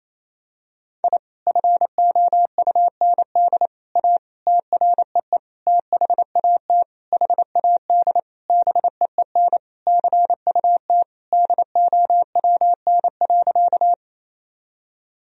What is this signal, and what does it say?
Morse code, heard as a keyed tone, about 28 wpm, 700 Hz. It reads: I FOUND A TREE THAT HAD BEEN CUT DOWN.
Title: Sentences from Top 1000 Words - Review Part 1 - 28wpm